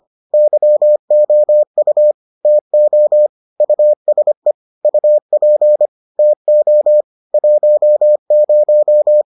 Morse code, heard as a keyed tone, about 25 wpm, 600 Hz. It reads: YOU TO USE UP TO 10